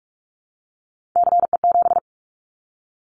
Morse code, heard as a keyed tone, about 44 wpm, 700 Hz. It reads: CE7
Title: Callsign Prefixes for Top 100 DXCC - 45wpm